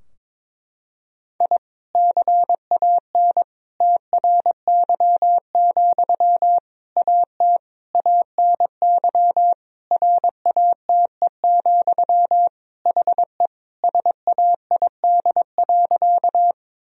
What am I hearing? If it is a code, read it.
Morse code, heard as a keyed tone, about 22 wpm, 700 Hz. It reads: I CAN TRY, AT ANY RATE, HE SAID.
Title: Sentences from Top 1000 Words - Review Part 4 - 22wpm